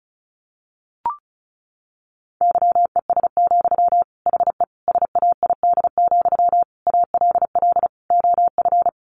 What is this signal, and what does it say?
Morse code, heard as a keyed tone, about 34 wpm, 700 Hz. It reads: YES, HE SAID, ALL OF
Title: Sentences from Top 500 Words 35wpm